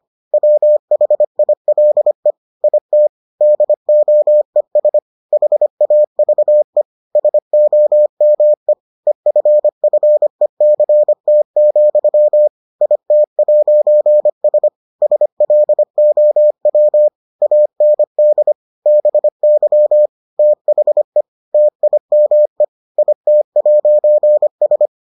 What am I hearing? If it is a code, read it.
Morse code, heard as a keyed tone, about 25 wpm, 600 Hz. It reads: WHILE IT DOES HAVE SOME EFFECT, IT'S SLOW AND BY THE TIME IT'S